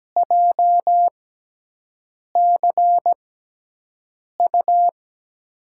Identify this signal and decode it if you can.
Morse code, heard as a keyed tone, about 17 wpm, spaced as for 6 wpm, 700 Hz. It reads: JCU
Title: USA Call Signs - 5 Characters - 6x Character Spacing - 17wpm